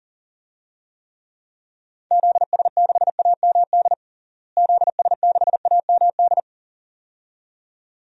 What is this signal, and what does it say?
Morse code, heard as a keyed tone, about 40 wpm, 700 Hz. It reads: ZS6AMD ZS6AMD
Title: DXCC-100 Call Signs - 6 Characters - Repeated 2x - 40wpm